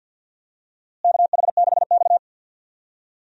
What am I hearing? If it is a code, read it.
Morse code, heard as a keyed tone, about 49 wpm, 700 Hz. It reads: KH6X
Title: DXCC-100 Call Signs - 4 Characters - Repeated 2x - 50wpm